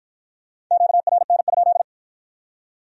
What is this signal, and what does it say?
Morse code, heard as a keyed tone, about 49 wpm, 700 Hz. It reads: QRN?